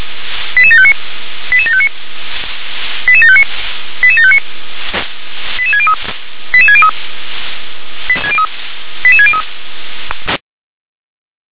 Five-tone selective call (ZVEI), heard as quick runs of five tones, tones of about 70 ms. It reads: 88759 88759 88759 88759 88603 88603 88603